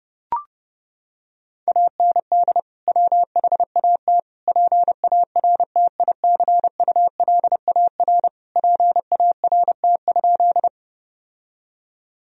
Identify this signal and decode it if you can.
Morse code, heard as a keyed tone, about 30 wpm, 700 Hz. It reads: AND WHAT PARTICULAR PART?
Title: Sentences from Top 1000 Words - Review Part 4 - 30wpm